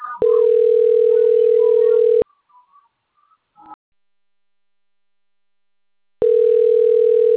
Ringback tone (North America), 440 + 480 Hz.
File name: rg-1600-unknown- - - .44141